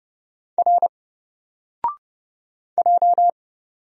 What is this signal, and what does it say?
Morse code, heard as a keyed tone, about 29 wpm, 700 Hz. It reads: R J
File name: Single-Letter-Number-30wpm